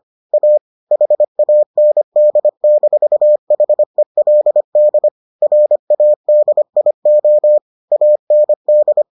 Morse code, heard as a keyed tone, about 25 wpm, 600 Hz. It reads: A HAND-HELD RADIO AND